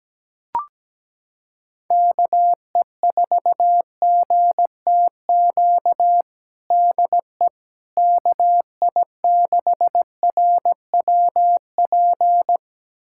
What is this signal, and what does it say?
Morse code, heard as a keyed tone, about 17 wpm, 700 Hz. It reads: KE4GTQ DE KI6RWP
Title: USA Callsign DE Callsign - 6 Characters - 17wpm